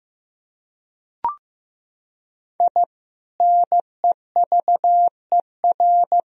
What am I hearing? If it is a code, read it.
Morse code, heard as a keyed tone, about 15 wpm, 700 Hz. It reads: I NEVER
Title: Sentences from Top 1000 Words - Review Part 1 - 15wpm